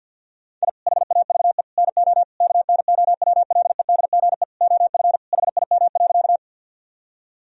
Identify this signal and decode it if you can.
Morse code, heard as a keyed tone, about 50 wpm, 700 Hz. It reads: I HAVE NO KNOWLEDGE OF HIM.
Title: Sentences from Top 1000 Words - Review Part 3 - 50wpm